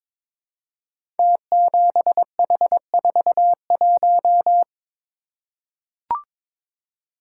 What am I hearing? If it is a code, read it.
Morse code, heard as a keyed tone, about 22 wpm, 700 Hz. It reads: T7H41
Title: Code Groups - 5 Letter:Numbers - 22wpm